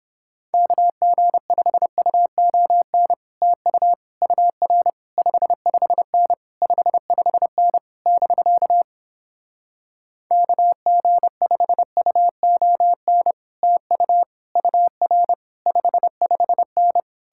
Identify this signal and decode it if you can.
Morse code, heard as a keyed tone, about 31 wpm, 700 Hz. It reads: KG5UON TU UR 55N 55N <BK> KG5UON TU UR 55N